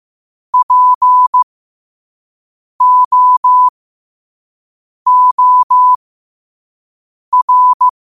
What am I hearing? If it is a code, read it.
Morse code, heard as a keyed tone, about 15 wpm, spaced as for 5 wpm, 1000 Hz. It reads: POOR
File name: qotdfiles / 5wpm